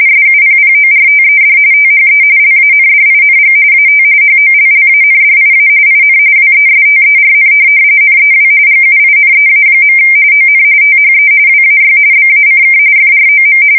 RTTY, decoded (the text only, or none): SUPPORTED THE LANDING AT HOLLANDIA IN WESTERN NEW GUINEA. ON THE WAY BACK FROM